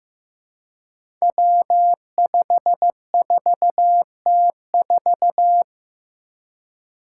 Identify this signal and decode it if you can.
Morse code, heard as a keyed tone, about 15 wpm, 700 Hz. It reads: W54T4
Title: Five Challenging Letter-Number Combinations 15wpm